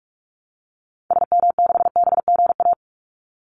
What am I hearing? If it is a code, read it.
Morse code, heard as a keyed tone, about 44 wpm, 700 Hz. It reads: SM6BGA
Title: Call Signs Repeated 2x - 45wpm